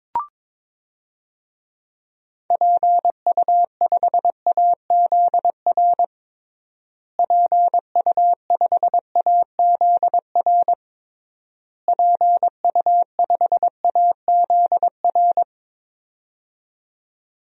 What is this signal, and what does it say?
Morse code, heard as a keyed tone, about 22 wpm, 700 Hz. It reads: PU5AZR PU5AZR PU5AZR
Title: DXCC-100 Call Signs - 6 Characters - Repeated 3x - 22wpm